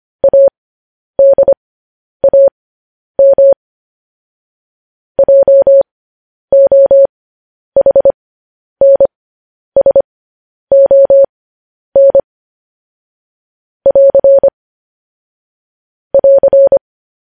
Morse code, heard as a keyed tone, about 25 wpm, spaced as for 10 wpm, 550 Hz. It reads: ADAM JOHNSON <AR> <AR>
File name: podcast / News headlines-1x-25x10